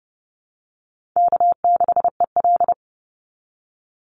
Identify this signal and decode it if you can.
Morse code, heard as a keyed tone, about 30 wpm, 700 Hz. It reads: K6EL
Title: NA SOTA Chaser Callsigns 30wpm